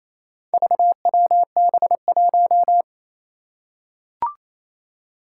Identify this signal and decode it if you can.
Morse code, heard as a keyed tone, about 27 wpm, 700 Hz. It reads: VWB1
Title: Four Challenging Letter-Number Combinations 28wpm